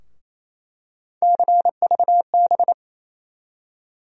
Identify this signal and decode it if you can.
Morse code, heard as a keyed tone, about 28 wpm, 700 Hz. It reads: CVB
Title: USA Call Sign Postfixes - 3 Letters 28wpm